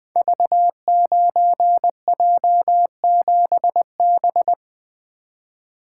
Morse code, heard as a keyed tone, about 20 wpm, 700 Hz. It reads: V9J7B
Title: Code Groups - 5 Letter:Numbers - 20wpm